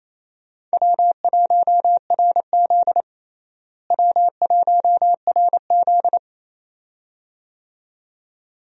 Morse code, heard as a keyed tone, about 28 wpm, 700 Hz. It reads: W1RZ W1RZ
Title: USA Call Signs - 4 Characters - Repeated 2x - 28wpm